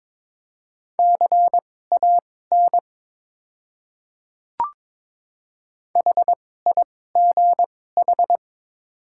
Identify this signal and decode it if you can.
Morse code, heard as a keyed tone, about 22 wpm, spaced as for 16 wpm, 700 Hz. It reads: CAN HIGH
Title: 2x Character Spacing - Top 200 Words - 22wpm